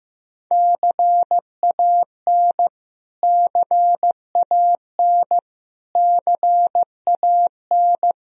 Morse code, heard as a keyed tone, about 16 wpm, 700 Hz. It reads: CAN CAN CAN
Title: Top 100 Words Repeated 15wpm